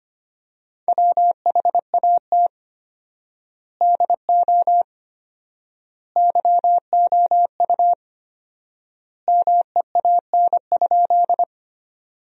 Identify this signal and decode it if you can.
Morse code, heard as a keyed tone, about 25 wpm, 700 Hz. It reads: WHAT DO YOU MEAN?
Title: 4x Word-Spacing - Sentences from Top 200 Words - 25wpm